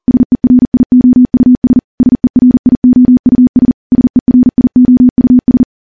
Morse code, heard as a keyed tone, about 40 wpm, 250 Hz. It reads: SERIOUS SERIOUS SERIOUS
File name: morse / morse-training-40WPM